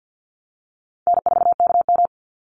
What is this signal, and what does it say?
Morse code, heard as a keyed tone, about 52 wpm, 700 Hz. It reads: N4KM
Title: Easy Call Signs Repeated 2x - 50wpm